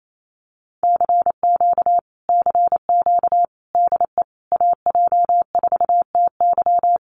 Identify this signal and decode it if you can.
Morse code, heard as a keyed tone, about 28 wpm, 700 Hz. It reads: CQ CQ DE AJ4TY
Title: Common QSO Phrases Encore 28wpm